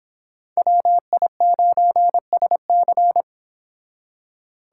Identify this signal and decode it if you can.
Morse code, heard as a keyed tone, about 25 wpm, 700 Hz. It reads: WI9SC